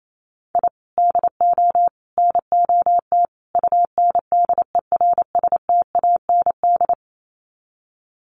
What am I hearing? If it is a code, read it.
Morse code, heard as a keyed tone, about 28 wpm, 700 Hz. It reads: I DO NOT UNDERSTAND
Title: Sets of 4 Words 28wpm